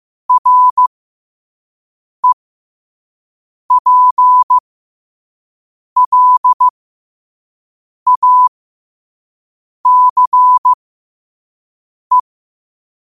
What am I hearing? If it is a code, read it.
Morse code, heard as a keyed tone, about 15 wpm, spaced as for 5 wpm, 1000 Hz. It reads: REPLACE
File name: qotdfiles / 5wpm